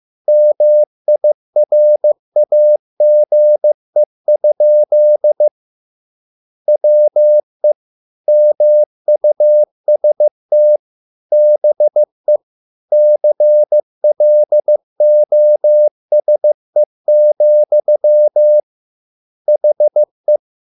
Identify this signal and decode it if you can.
Morse code, heard as a keyed tone, about 15 wpm, 600 Hz. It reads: MIRAGE? WE MUST BE CLOSE, HE